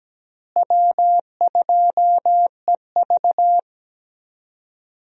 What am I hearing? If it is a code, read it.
Morse code, heard as a keyed tone, about 17 wpm, 700 Hz. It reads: W2EV
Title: USA Call Signs - 4 Characters - Repeated 2x - 17wpm